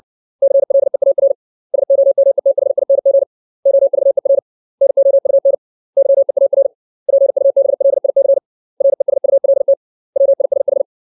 Morse code, heard as a keyed tone, about 62 wpm, 550 Hz. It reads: CLEAR SOMETHING OVER NOUN YEAR PUBLIC RESULT WISH
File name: 632 words / 632 Words 60WPM